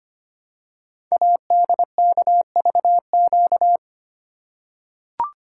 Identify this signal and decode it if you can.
Morse code, heard as a keyed tone, about 25 wpm, 700 Hz. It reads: ADKVQ
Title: Code Groups - 5 Letters - 25wpm